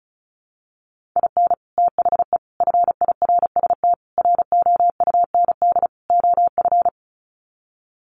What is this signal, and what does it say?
Morse code, heard as a keyed tone, about 34 wpm, 700 Hz. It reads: IN THE FIRST ROUND OF
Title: Sets of 5 Words Encore 35wpm